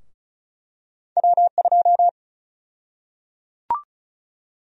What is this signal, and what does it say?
Morse code, heard as a keyed tone, about 34 wpm, 700 Hz. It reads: W2
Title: USA Call Sign Prefixes - 35wpm